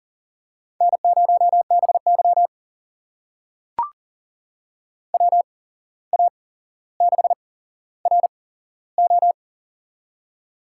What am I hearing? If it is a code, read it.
Morse code, heard as a keyed tone, about 40 wpm, spaced as for 16 wpm, 690 Hz. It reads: N0BY WA6RO
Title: USA Call Signs - 8x Character Spacing - 40wpm